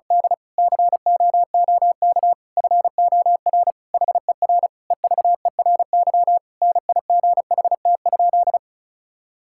Morse code, heard as a keyed tone, about 36 wpm, 700 Hz. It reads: D COOK FOR HER EVERY NIGHT?
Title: Sentences from Top 1000 Words - Review Part 5 - 35wpm